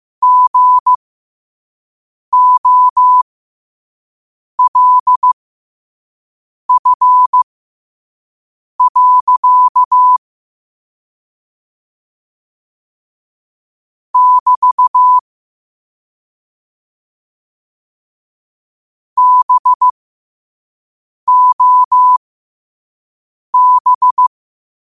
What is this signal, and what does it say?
Morse code, heard as a keyed tone, about 15 wpm, spaced as for 5 wpm, 1000 Hz. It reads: GOLF. <BT> BOB
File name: qotdfiles / 5wpm